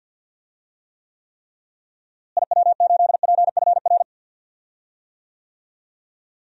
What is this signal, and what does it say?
Morse code, heard as a keyed tone, about 46 wpm, 700 Hz. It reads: IW8PFR
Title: DXCC-100 Call Signs - 6 Characters - Repeated 3x - 50wpm